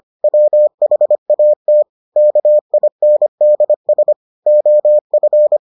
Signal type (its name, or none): Morse code